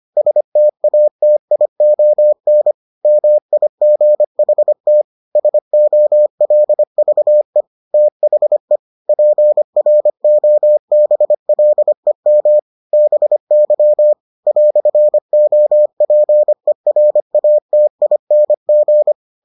Morse code, heard as a keyed tone, about 25 wpm, 600 Hz. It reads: STATION MIGHT SOLVE THE PROBLEM BY "OPERATING